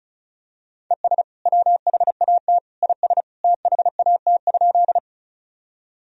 Morse code, heard as a keyed tone, about 34 wpm, 700 Hz. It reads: ES WHAT IS THAT?
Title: Sentences from Top 100 Words and Abbreviations 35wpm